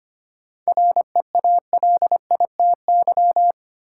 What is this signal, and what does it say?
Morse code, heard as a keyed tone, about 25 wpm, 700 Hz. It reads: REALITY
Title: Sentences from Top 1200 Words - New Word Drill Alt - 25wpm